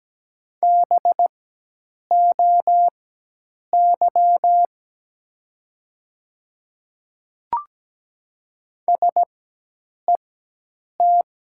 Morse code, heard as a keyed tone, about 17 wpm, spaced as for 8 wpm, 700 Hz. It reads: BOY SET
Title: 4x Character Spacing - Top 200 Words - 17wpm